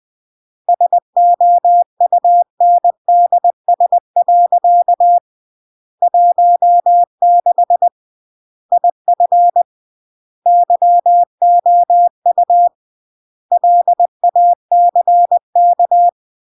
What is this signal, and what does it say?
Morse code, heard as a keyed tone, about 20 wpm, 700 Hz. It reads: SOUNDS. 16 IF YOU LACK